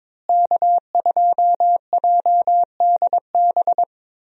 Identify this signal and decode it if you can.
Morse code, heard as a keyed tone, about 21 wpm, 700 Hz. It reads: K2JDB
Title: USA Call Signs - 5 Characters - Repeated 3x - 22wpm